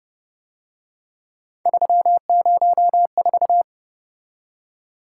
Morse code, heard as a keyed tone, about 31 wpm, 700 Hz. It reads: 304
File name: Speed-Racing - Three Letter-Number Combinations - 25wpm